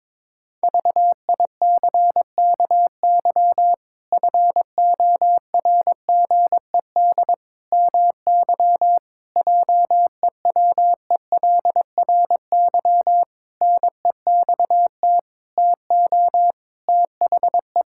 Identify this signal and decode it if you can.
Morse code, heard as a keyed tone, about 22 wpm, 700 Hz. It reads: VICKY FORGED MY JEWELRY NEXT TO THE
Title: Pangram Challenge v2 - 22wpm